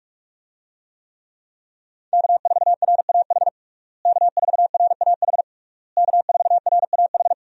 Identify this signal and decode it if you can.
Morse code, heard as a keyed tone, about 46 wpm, 700 Hz. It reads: K4RAH K4RAH K4RAH
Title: USA Call Signs - 5 Characters - Repeated 3x - 45wpm